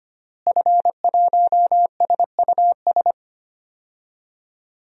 Morse code, heard as a keyed tone, about 25 wpm, 700 Hz. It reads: F1SUS
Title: DXCC-100 Call Signs - 5 Characters - Repeated 3x - 25wpm